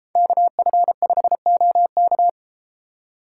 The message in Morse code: KF5OK

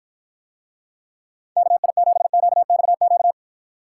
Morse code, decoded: KI7QXQ

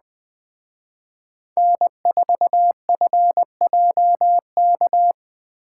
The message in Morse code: N4FJK

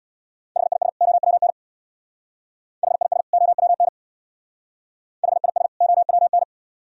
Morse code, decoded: HIS OWN HIS OWN HIS OWN

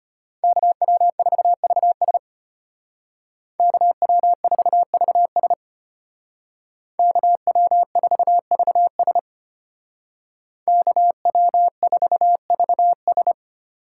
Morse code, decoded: KW4VS KW4VS KW4VS KW4VS

